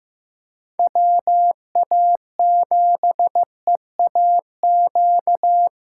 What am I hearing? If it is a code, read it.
Morse code, heard as a keyed tone, about 15 wpm, 700 Hz. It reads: WA7EAQ
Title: USA Callsign DE Callsign 15wpm